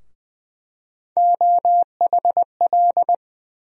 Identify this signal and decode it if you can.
Morse code, heard as a keyed tone, about 20 wpm, 700 Hz. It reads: OHL
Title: USA Call Sign Postfixes - 3 Letters 20wpm